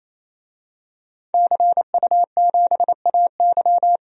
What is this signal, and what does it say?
Morse code, heard as a keyed tone, about 28 wpm, 700 Hz. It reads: CU7AY
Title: DXCC-100 Call Signs - 5 Characters - Repeated 3x - 28wpm